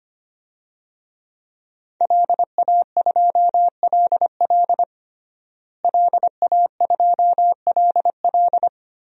Morse code, heard as a keyed tone, about 25 wpm, 700 Hz. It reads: LA2LL LA2LL
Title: DXCC-100 Call Signs - 5 Characters - Repeated 2x - 25wpm